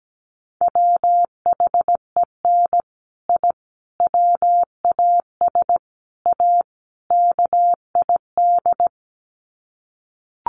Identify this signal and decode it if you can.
Morse code, heard as a keyed tone, about 17 wpm, 700 Hz. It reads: WHEN I WAS A KID
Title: Sets of 5 Words 17wpm